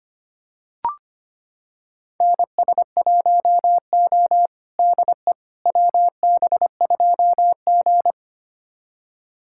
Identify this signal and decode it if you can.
Morse code, heard as a keyed tone, about 25 wpm, 700 Hz. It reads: NS1O DE WB2G
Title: USA Callsign DE Callsign - 4 Characters - 25wpm